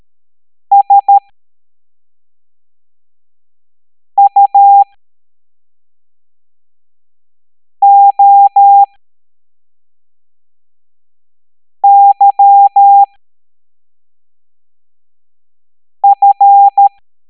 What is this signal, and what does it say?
Morse code, heard as a keyed tone, about 13 wpm, spaced as for 3 wpm, 800 Hz. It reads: SUOYF